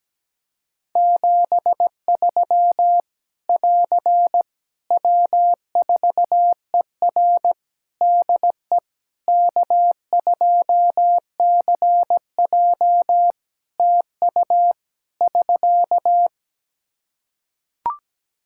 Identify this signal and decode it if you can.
Morse code, heard as a keyed tone, about 17 wpm, 700 Hz. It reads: 73 <AR> W4ER DE K2CJ TU <SK>
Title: Practice QSO - Exchange 4 - Ending - 17wpm